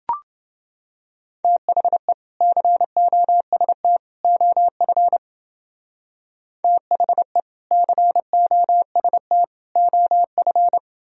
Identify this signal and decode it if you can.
Morse code, heard as a keyed tone, about 29 wpm, 700 Hz. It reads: THE COST OF THE COST OF